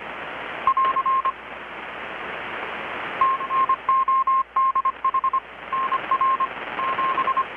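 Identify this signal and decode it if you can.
Morse code, heard as a keyed tone, about 25 wpm, 1060 Hz, with radio noise. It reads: <AR> TNODH NR 7
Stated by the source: Morse from KPH's Night of Nights XX maritime radiotelegraphy event, received in Chiapas, Mexico about 3,800 km away on a Kenwood R-600 with a grounded 10 m long-wire antenna.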